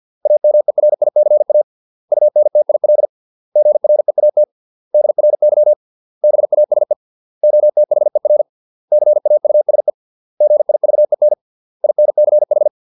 Morse code, heard as a keyed tone, about 50 wpm, 600 Hz. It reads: AMERICA UNTIL GREAT DRY BASE OTHER CAUSE GIVEN INCH